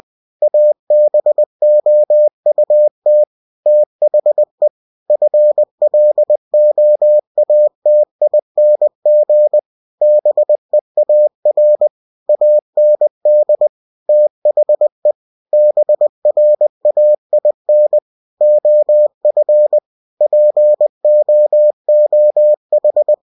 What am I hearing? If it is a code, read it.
Morse code, heard as a keyed tone, about 20 wpm, 600 Hz. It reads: ABOUT THE FLOATING BEAR AND THE BRAIN OF POOH
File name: cw-books / winnie-the-pooh / winnie-20wpm-page-0103